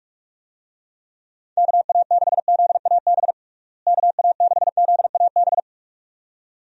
Morse code, heard as a keyed tone, about 45 wpm, 700 Hz. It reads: KA6ZAB KA6ZAB
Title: USA Call Signs - 6 Characters - Repeated 2x - 45wpm